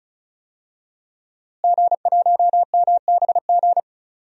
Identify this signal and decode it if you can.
Morse code, heard as a keyed tone, about 36 wpm, 690 Hz. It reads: G1MBG
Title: DXCC-100 Call Signs - 5 Characters - Repeated 3x - 35wpm